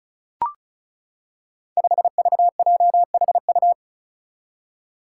Morse code, heard as a keyed tone, about 36 wpm, 700 Hz. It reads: 5VJHU
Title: Five Challenging Letter-Number Combinations 35wpm